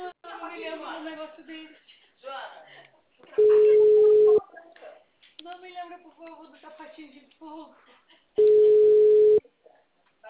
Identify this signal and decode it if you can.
Ringback tone (Europe, 425 Hz), with speech.